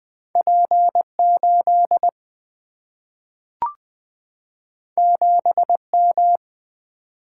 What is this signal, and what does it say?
Morse code, heard as a keyed tone, about 20 wpm, 700 Hz. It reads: P8 7M
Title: Two Letter-Number Combinations - 20wpm